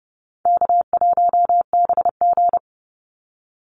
Morse code, heard as a keyed tone, about 30 wpm, 700 Hz. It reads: K1BG
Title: Easy Call Signs Repeated 3x - 30wpm